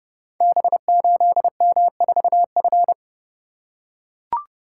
Morse code, heard as a keyed tone, about 30 wpm, 700 Hz. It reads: B8M4F
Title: Code Groups - 5 Letter:Numbers - 30wpm